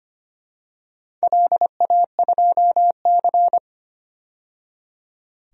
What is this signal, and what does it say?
Morse code, heard as a keyed tone, about 25 wpm, 700 Hz. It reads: LA2C
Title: DXCC-100 Call Signs - 4 Characters - 25wpm